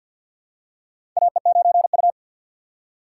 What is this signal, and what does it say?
Morse code, heard as a keyed tone, about 48 wpm, 690 Hz. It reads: AE9U
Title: USA Call Signs - 4x Character Spacing - 50wpm